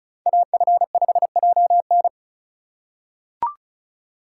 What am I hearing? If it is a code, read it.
Morse code, heard as a keyed tone, about 34 wpm, 690 Hz. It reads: AF5JN
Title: USA Call Signs - 5 Characters - 3x Character Spacing - 35wpm